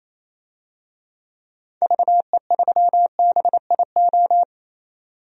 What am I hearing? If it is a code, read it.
Morse code, heard as a keyed tone, about 28 wpm, 700 Hz. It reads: VE3BIO
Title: DXCC-100 Call Signs - 6 Characters - 28wpm